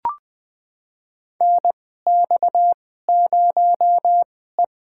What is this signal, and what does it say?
Morse code, heard as a keyed tone, about 20 wpm, spaced as for 15 wpm, 700 Hz. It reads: NX0E
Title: USA Call Signs - 4 Characters - 2x Character Spacing - 20wpm